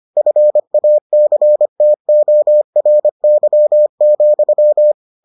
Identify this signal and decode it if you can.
Morse code, heard as a keyed tone, about 25 wpm, 600 Hz. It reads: FACTORY,